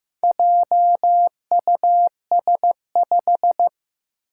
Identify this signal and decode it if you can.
Morse code, heard as a keyed tone, about 15 wpm, 700 Hz. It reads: JUS5
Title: Four Challenging Letter-Number Combinations 15wpm